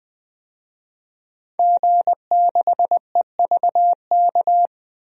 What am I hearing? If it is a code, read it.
Morse code, heard as a keyed tone, about 21 wpm, 700 Hz. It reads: G6EVK
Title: DXCC-100 Call Signs - 5 Characters - 20wpm